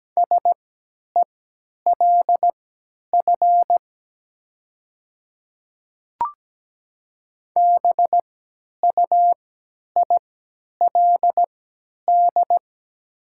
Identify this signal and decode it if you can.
Morse code, heard as a keyed tone, about 17 wpm, spaced as for 10 wpm, 700 Hz. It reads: SELF BUILD
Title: 3x Character Spacing - Top 200 Words - 17wpm